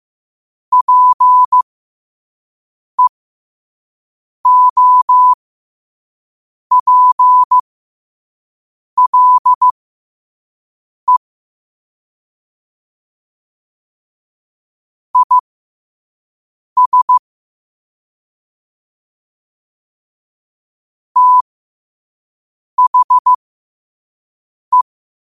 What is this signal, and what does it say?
Morse code, heard as a keyed tone, about 15 wpm, spaced as for 5 wpm, 1000 Hz. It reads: PEOPLE IS THE
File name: qotdfiles / 5wpm